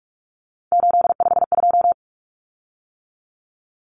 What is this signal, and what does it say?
Morse code, heard as a keyed tone, about 44 wpm, 700 Hz. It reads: 852
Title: Common Cut Numbers - 3-Digit 45wpm